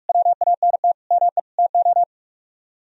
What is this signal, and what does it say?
Morse code, heard as a keyed tone, about 47 wpm, 700 Hz. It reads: WANT ME TO